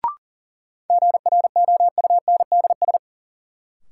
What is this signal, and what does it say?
Morse code, heard as a keyed tone, about 40 wpm, 700 Hz. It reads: GROUNDS
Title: Sentences from Top 1300 Words - New Word Drill - 40wpm